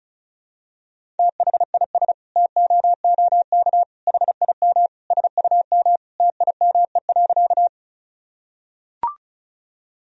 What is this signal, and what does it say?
Morse code, heard as a keyed tone, about 36 wpm, 700 Hz. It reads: THIS TOOK HIM SUM TIME.
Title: Sentences from Top 300 Words and Abbreviations 35wpm